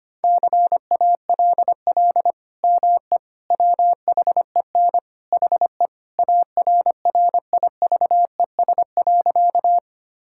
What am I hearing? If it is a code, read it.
Morse code, heard as a keyed tone, about 25 wpm, 700 Hz. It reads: CALL ME WHEN HE ARRIVES.
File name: practice / 25wpm / Sentences with 2-Word Phrases - 25wpm